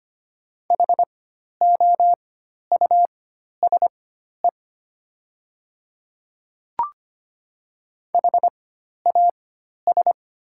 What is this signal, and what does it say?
Morse code, heard as a keyed tone, about 25 wpm, spaced as for 12 wpm, 700 Hz. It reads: HOUSE HAS
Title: 4x Character Spacing - Top 200 Words - 25wpm